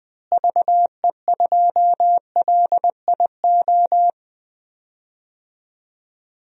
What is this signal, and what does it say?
Morse code, heard as a keyed tone, about 20 wpm, 700 Hz. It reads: VE2LIO
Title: DXCC-100 Call Signs - 6 Characters - Repeated 2x - 20wpm